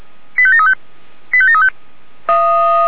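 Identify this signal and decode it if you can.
Five-tone selective call (ZVEI), heard as quick runs of five tones, tones of about 70 ms. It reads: 85735 85735
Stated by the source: ZVEI five-tone call 85865 received by SDR.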